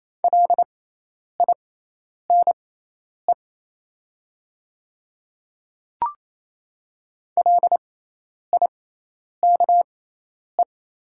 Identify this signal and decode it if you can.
Morse code, heard as a keyed tone, about 28 wpm, spaced as for 10 wpm, 700 Hz. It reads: LINE LIKE
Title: 6x Character Spacing - Top 200 Words - 28wpm